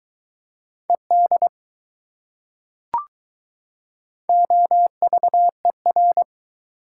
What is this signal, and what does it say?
Morse code, heard as a keyed tone, about 22 wpm, 700 Hz. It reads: ED OVER